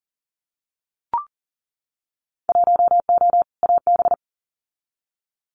Morse code, heard as a keyed tone, about 40 wpm, 700 Hz. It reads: 1O AB
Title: Winter Field Day Exchanges Encore 40wpm